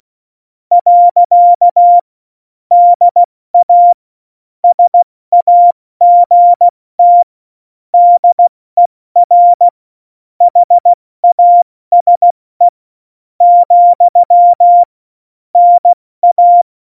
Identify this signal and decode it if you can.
Morse code, heard as a keyed tone, about 16 wpm, 700 Hz. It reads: . DA SAGT DER HASE , NA
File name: Morsekurs / Witze / Witz 28. August 2018, 70 F80